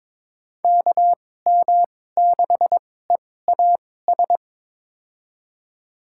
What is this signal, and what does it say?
Morse code, heard as a keyed tone, about 22 wpm, spaced as for 16 wpm, 700 Hz. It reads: KM6EAS
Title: USA Call Signs - 6 Characters - 2x Character Spacing - 22wpm